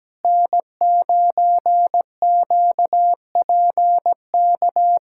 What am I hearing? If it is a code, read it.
Morse code, heard as a keyed tone, about 17 wpm, 700 Hz. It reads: N9QPK
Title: USA Callsign DE Callsign 17wpm